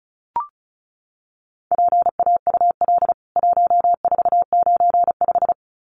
Morse code, heard as a keyed tone, about 34 wpm, 700 Hz. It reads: PAUL 1495
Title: CWT Exchange 35wpm